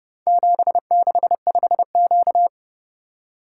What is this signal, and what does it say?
Morse code, heard as a keyed tone, about 31 wpm, 700 Hz. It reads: 765Q